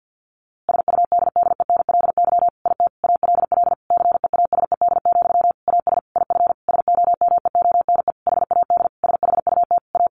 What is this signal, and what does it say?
Morse code, heard as a keyed tone, about 50 wpm, 700 Hz. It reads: SUDDENLY IT ALL CEASED, AS IF SOMEONE HAD SHUT A
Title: A Sound Of Thunder 50wpm